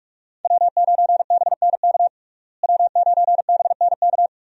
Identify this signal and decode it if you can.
Morse code, heard as a keyed tone, about 42 wpm, 700 Hz. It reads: W9BNK W9BNK
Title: USA Call Signs - 5 Characters - Repeated 2x - 45wpm